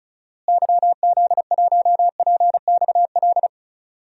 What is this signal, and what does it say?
Morse code, heard as a keyed tone, about 36 wpm, 700 Hz. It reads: YZ1PXL
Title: DXCC-100 Call Signs - 6 Characters - Repeated 3x - 35wpm